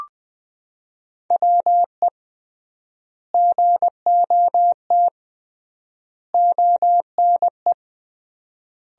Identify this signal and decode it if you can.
Morse code, heard as a keyed tone, about 20 wpm, 700 Hz. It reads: WE GOT ONE
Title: 3x Word-Spacing - Sentences from Top 300 Words - 20wpm